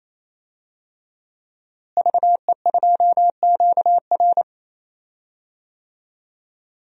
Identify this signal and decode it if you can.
Morse code, heard as a keyed tone, about 28 wpm, 700 Hz. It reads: VE2QR
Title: DXCC-100 Call Signs - 5 Characters - Repeated 3x - 28wpm